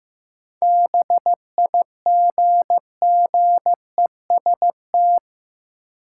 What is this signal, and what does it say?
Morse code, heard as a keyed tone, about 15 wpm, 700 Hz. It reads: BIGGEST
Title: Sets of 4 Words - New Word Drill - 15wpm